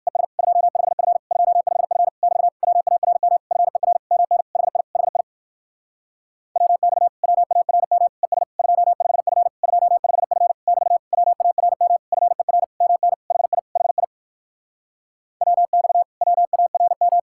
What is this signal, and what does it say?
Morse code, heard as a keyed tone, about 65 wpm, 700 Hz. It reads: ES 25F 25F <BT> WARM FER MN HI HI WX WARM ES 25F 25F <BT> WARM FER MN HI HI WX WARM